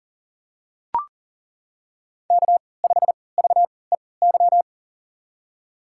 Morse code, heard as a keyed tone, about 40 wpm, spaced as for 23 wpm, 700 Hz. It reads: K5VEY
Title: USA Call Signs - 5 Characters - 3x Character Spacing - 40wpm